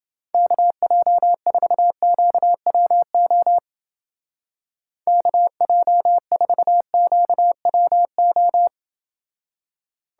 Morse code, heard as a keyed tone, about 29 wpm, 700 Hz. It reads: KJ4QWO KJ4QWO